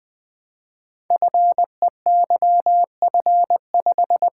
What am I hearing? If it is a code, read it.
Morse code, heard as a keyed tone, about 20 wpm, 700 Hz. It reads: FEYF5
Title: Code Groups - 5 Letter:Numbers - 20wpm